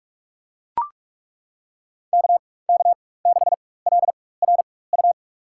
Morse code, heard as a keyed tone, about 42 wpm, spaced as for 21 wpm, 700 Hz. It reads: KK6LRU
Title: USA Call Signs - 6 Characters - 4x Character Spacing - 45wpm